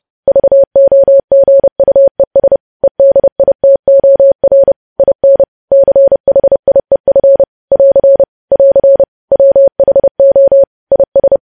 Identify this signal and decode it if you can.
Morse code, heard as a keyed tone, about 30 wpm, 550 Hz. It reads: VOGUES EDITOR IN CHIEF <AR> <AR> WHO IS
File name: podcast / News headlines-1x-30x30